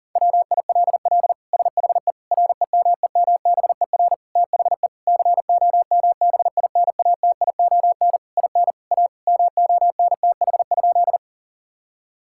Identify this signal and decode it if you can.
Morse code, heard as a keyed tone, about 40 wpm, 700 Hz. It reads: WILL SHE REMEMBER THE COMBINATION IN A MONTH?
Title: Sentences from Top 1000 Words - Review Part 6 - 40wpm